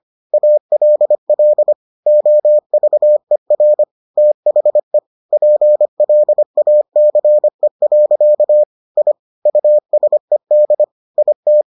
Morse code, heard as a keyed tone, about 25 wpm, 600 Hz. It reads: ALL OVER THE PLACE. I USED IT